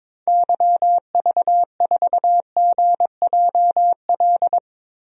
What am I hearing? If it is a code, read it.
Morse code, heard as a keyed tone, about 21 wpm, 700 Hz. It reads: YV4GJL
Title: DXCC-100 Call Signs - 6 Characters - Repeated 2x - 22wpm